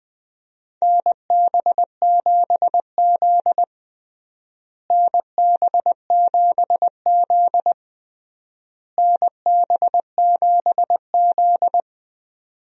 Morse code, heard as a keyed tone, about 20 wpm, 700 Hz. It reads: NB7Z NB7Z NB7Z
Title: USA Call Signs - 4 Characters - Repeated 3x - 20wpm